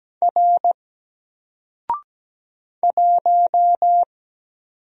Morse code, heard as a keyed tone, about 17 wpm, 700 Hz. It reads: R 1
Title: Single Character (Letter:Number:Punctuation:Prosign) 17wpm